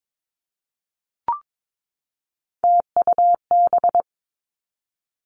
Morse code, heard as a keyed tone, about 22 wpm, 700 Hz. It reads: TUB
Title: Three Challenging Letter-Number Combinations 22wpm